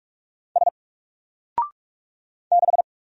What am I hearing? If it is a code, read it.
Morse code, heard as a keyed tone, about 46 wpm, 700 Hz. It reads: S 6
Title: Single Letter-Number 45wpm